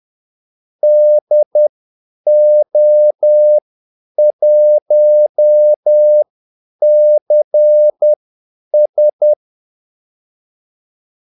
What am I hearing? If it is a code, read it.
Morse code, heard as a keyed tone, about 10 wpm, spaced as for 8 wpm, 600 Hz. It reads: DO1CS
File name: Callsigns-10-8wpm